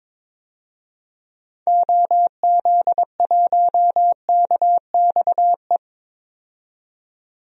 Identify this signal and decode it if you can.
Morse code, heard as a keyed tone, about 22 wpm, 700 Hz. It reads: OZ1KXE
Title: DXCC-100 Call Signs - 6 Characters - 22wpm